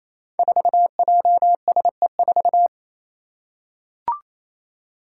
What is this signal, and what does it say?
Morse code, heard as a keyed tone, about 28 wpm, 700 Hz. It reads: 4JSE4
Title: Five Challenging Letter-Number Combinations 28wpm